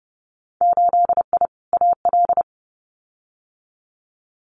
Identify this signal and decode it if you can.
Morse code, heard as a keyed tone, about 30 wpm, 700 Hz. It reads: 8I AL
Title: Winter Field Day Exchanges Encore 30wpm